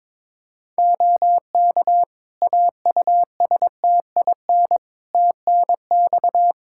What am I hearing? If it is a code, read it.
Morse code, heard as a keyed tone, about 22 wpm, 700 Hz. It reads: OK AUSTIN TNX
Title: Practice QSO - Exchange 4 - Ending - 22wpm